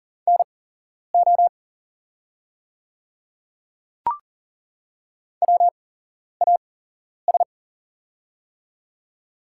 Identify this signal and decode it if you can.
Morse code, heard as a keyed tone, about 40 wpm, spaced as for 11 wpm, 700 Hz. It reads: NO WAS
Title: 8x Character Spacing - Top 200 Words - 40wpm